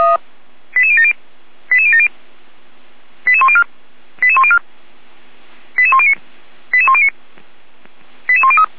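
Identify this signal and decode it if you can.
Five-tone selective call (ZVEI), heard as quick runs of five tones, tones of about 70 ms. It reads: 88070 88070 88104 88104 88109 88109 88103
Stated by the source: Five-tone ZVEI selective call received on an SDR.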